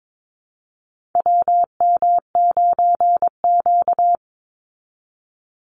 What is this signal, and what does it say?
Morse code, heard as a keyed tone, about 22 wpm, 700 Hz. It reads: WM9Q
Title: Call Signs Repeated 3x - 22wpm